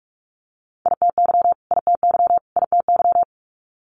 Morse code, heard as a keyed tone, about 46 wpm, 700 Hz. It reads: ITY ITY ITY
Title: Multi-letter Phonemes Repeated 3x 45wpm